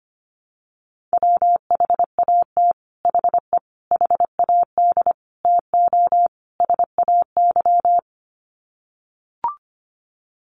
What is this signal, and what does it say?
Morse code, heard as a keyed tone, about 25 wpm, 700 Hz. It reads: WHAT HE HAD TO SAY
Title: Sets of 5 Words Encore2 25wpm